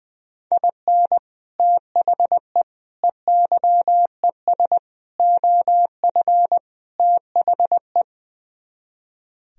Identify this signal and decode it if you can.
Morse code, heard as a keyed tone, about 20 wpm, 700 Hz. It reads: IN THE EYES OF THE
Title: Speed-Racing - Sets of 5 Words 20wpm